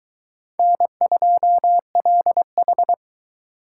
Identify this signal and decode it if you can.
Morse code, heard as a keyed tone, about 23 wpm, 700 Hz. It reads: N2LH